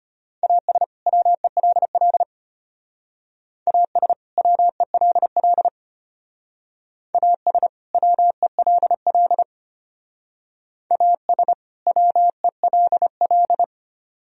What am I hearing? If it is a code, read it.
Morse code, heard as a keyed tone, about 32 wpm, 700 Hz. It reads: AS WELL AS WELL AS WELL AS WELL